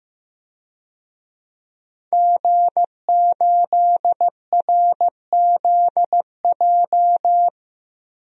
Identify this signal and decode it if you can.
Morse code, heard as a keyed tone, about 15 wpm, 700 Hz. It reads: G8RZJ
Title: DXCC-100 Call Signs - 5 Characters - 15wpm